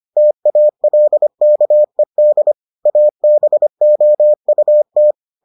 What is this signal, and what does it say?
Morse code, heard as a keyed tone, about 25 wpm, 600 Hz. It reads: TALKED ABOUT